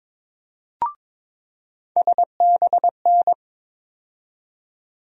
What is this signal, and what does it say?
Morse code, heard as a keyed tone, about 22 wpm, 700 Hz. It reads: SBN
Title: USA Call Sign Postfixes - 3 Letters 22wpm